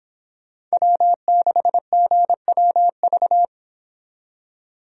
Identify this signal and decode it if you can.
Morse code, heard as a keyed tone, about 28 wpm, 700 Hz. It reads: W6GWV